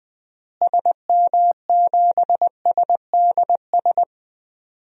Morse code, heard as a keyed tone, about 20 wpm, 700 Hz. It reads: SM7SDS
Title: DXCC-100 Call Signs - 6 Characters - Repeated 3x - 20wpm